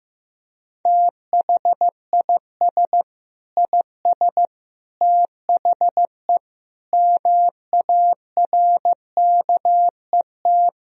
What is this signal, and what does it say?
Morse code, heard as a keyed tone, about 15 wpm, 700 Hz. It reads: THIS IS THE MARKET
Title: Sentences from Top 1000 Words - Review Part 3 - 15wpm